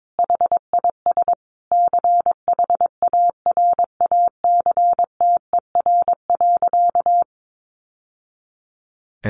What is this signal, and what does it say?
Morse code, heard as a keyed tone, about 22 wpm, 700 Hz. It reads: HIS CHARACTER.
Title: Sentences from Top 1200 Words 22wpm